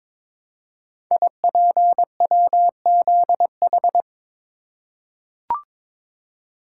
Morse code, heard as a keyed tone, about 22 wpm, 700 Hz. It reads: IPWZH
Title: Code Groups - 5 Letters - 22wpm